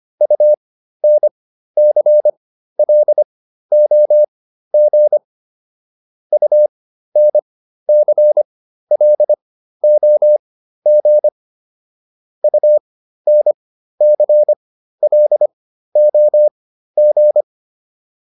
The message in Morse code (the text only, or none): UNCLOG UNCLOG UNCLOG